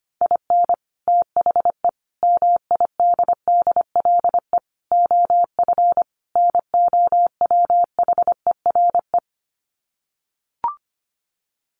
Morse code, heard as a keyed tone, about 25 wpm, 700 Hz. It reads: IN THE MIDDLE OF NOWHERE
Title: Sets of 5 Words Encore 25wpm